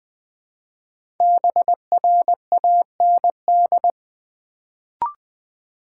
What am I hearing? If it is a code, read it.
Morse code, heard as a keyed tone, about 20 wpm, 700 Hz. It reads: BRAND
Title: Build A Word - 1 Letter At A Time - 5-Letter Words - 20wpm